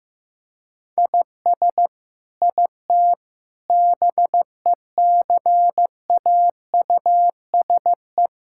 Morse code, heard as a keyed tone, about 15 wpm, 700 Hz. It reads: IS IT BECAUSE
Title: Sentences from Top 1000 Words - Review Part 5 - 15wpm